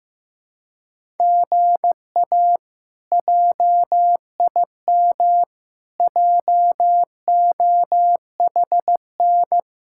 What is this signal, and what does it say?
Morse code, heard as a keyed tone, about 15 wpm, 700 Hz. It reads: GA JIM JOHN
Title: K1USN SST QSO Practice - 15wpm